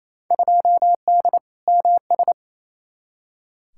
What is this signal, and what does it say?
Morse code, heard as a keyed tone, about 28 wpm, 700 Hz. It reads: 2D MS
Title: Field Day Exchanges Encore - 28wpm